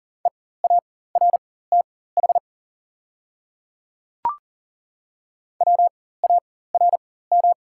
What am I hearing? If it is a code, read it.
Morse code, heard as a keyed tone, about 40 wpm, spaced as for 19 wpm, 690 Hz. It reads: EARTH WARM
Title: 4x Character Spacing - Top 500 Words - 40wpm